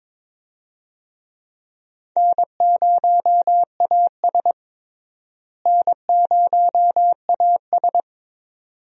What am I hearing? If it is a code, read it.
Morse code, heard as a keyed tone, about 22 wpm, 700 Hz. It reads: N0AS N0AS
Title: USA Call Signs - 4 Characters - Repeated 2x - 22wpm